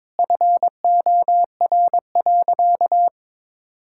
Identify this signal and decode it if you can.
Morse code, heard as a keyed tone, about 22 wpm, 700 Hz. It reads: FOR.
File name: Sentences with 7-Word Phrases - 22wpm